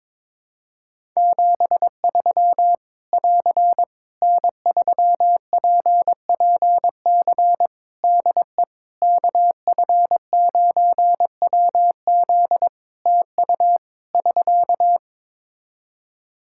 Morse code, heard as a keyed tone, about 21 wpm, 700 Hz. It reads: 73 <AR> N3PPC DE KF9WZ TU <SK>
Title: K1USN SST QSO Practice 22wpm